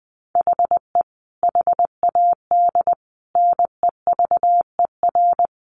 Morse code, heard as a keyed tone, about 19 wpm, 700 Hz. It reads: HE HAD NEVER
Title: Sentences from Top 1300 Words 20wpm